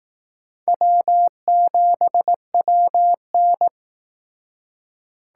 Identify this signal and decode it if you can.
Morse code, heard as a keyed tone, about 19 wpm, 700 Hz. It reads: W7WN